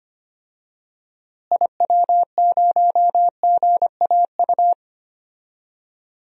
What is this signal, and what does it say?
Morse code, heard as a keyed tone, about 25 wpm, 700 Hz. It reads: IW0GAU
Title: DXCC-100 Call Signs - 6 Characters - Repeated 3x - 25wpm